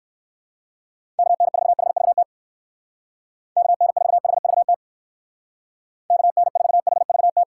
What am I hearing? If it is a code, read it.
Morse code, heard as a keyed tone, about 57 wpm, 700 Hz. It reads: KN4HVT KN4HVT KN4HVT